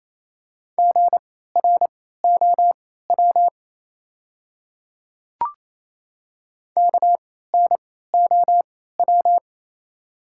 Morse code, heard as a keyed tone, about 28 wpm, spaced as for 16 wpm, 700 Hz. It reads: GROW KNOW
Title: 3x Character Spacing - Top 300 Words - 28wpm